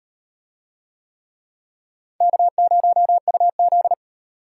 Morse code, heard as a keyed tone, about 38 wpm, 700 Hz. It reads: K0UZ